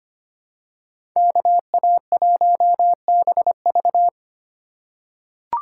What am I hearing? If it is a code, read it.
Morse code, heard as a keyed tone, about 25 wpm, 700 Hz. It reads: KA1BV
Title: USA Call Signs - 5 Characters - 4x Character Spacing - 25wpm